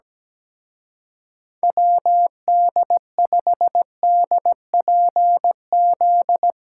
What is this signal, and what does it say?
Morse code, heard as a keyed tone, about 17 wpm, 700 Hz. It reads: WD5DPZ
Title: Speed Racing - USA 2x3 Call Signs 17wpm